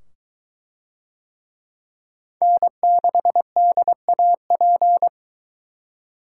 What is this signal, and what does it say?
Morse code, heard as a keyed tone, about 20 wpm, 700 Hz. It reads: N6DAP